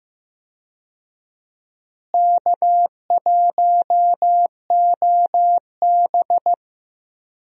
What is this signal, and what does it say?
Morse code, heard as a keyed tone, about 15 wpm, 700 Hz. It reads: K1OB
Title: USA Call Signs - 4 Characters - Repeated 2x - 15wpm